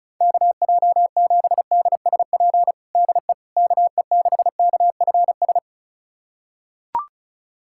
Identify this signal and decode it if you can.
Morse code, heard as a keyed tone, about 36 wpm, 700 Hz. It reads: KJ7DSP DE KE6KFS
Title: USA Callsign DE Callsign - 6 Characters - 35wpm